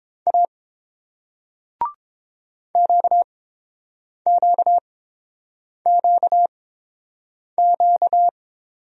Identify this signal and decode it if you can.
Morse code, heard as a keyed tone, about 30 wpm, spaced as for 7 wpm, 700 Hz. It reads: A QQQQ